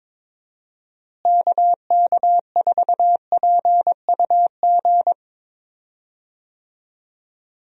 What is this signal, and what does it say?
Morse code, heard as a keyed tone, about 22 wpm, 700 Hz. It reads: KK4PUG
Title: USA Call Signs - 6 Characters - Repeated 2x - 22wpm